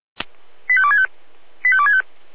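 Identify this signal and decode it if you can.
Five-tone selective call (ZVEI), heard as quick runs of five tones, tones of about 70 ms. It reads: 85265 85265
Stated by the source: ZVEI five-tone call received via SDR.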